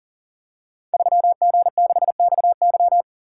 Morse code, heard as a keyed tone, about 42 wpm, 690 Hz. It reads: 3G6XY
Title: Code Groups - 5 Letter:Numbers - 40wpm